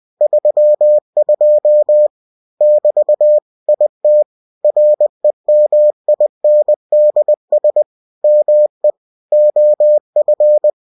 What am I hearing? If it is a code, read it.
Morse code, heard as a keyed tone, about 20 wpm, 600 Hz. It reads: 32 <BT> IT REMINDS ME OF
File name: cw-books / winnie-the-pooh / winnie-20wpm-page-0032